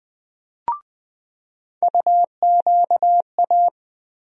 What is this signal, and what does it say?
Morse code, heard as a keyed tone, about 20 wpm, 700 Hz. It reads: UQA
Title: USA Call Sign Postfixes - 3 Letters 20wpm